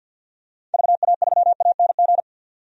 Morse code, heard as a keyed tone, about 51 wpm, 700 Hz. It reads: VA3ANG
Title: DXCC-100 Call Signs - 6 Characters - 50wpm